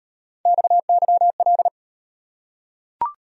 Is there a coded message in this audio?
Yes: Morse code from 0.5 to 1.7 s, heard as a keyed tone, about 33 wpm, 700 Hz. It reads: XYL